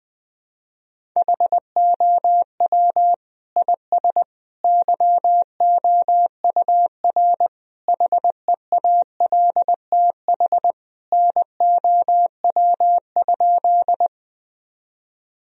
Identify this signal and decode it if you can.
Morse code, heard as a keyed tone, about 20 wpm, 700 Hz. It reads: HOW IS YOUR HEALTH NOW?
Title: Sentences from Top 1000 Words - Review Part 3 - 20wpm